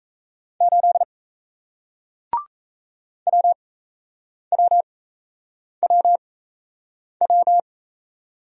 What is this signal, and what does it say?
Morse code, heard as a keyed tone, about 38 wpm, spaced as for 8 wpm, 700 Hz. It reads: 8 WWWW